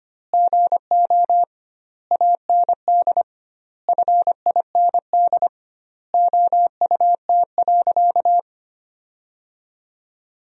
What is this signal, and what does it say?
Morse code, heard as a keyed tone, about 25 wpm, 700 Hz. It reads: GO AND FIND OUT.
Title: 2x Word-Spacing - Sentences from Top 200 Words - 25wpm